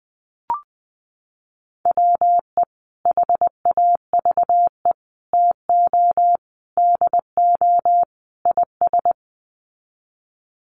Morse code, heard as a keyed tone, about 20 wpm, 700 Hz. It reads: WE HAVE TO DO IS
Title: Sets of 5 Words Encore 20wpm